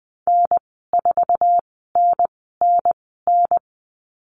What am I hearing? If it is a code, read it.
Morse code, heard as a keyed tone, about 20 wpm, spaced as for 14 wpm, 700 Hz. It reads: N4NNN
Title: 076 Lesson 13g - Letter L - Call Signs 20wpm